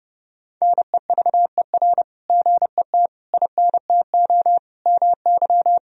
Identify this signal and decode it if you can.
Morse code, heard as a keyed tone, about 30 wpm, 700 Hz. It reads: NEVER GET INTO MY
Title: Sentences from Top 1000 Words - Review Part 3 - 30wpm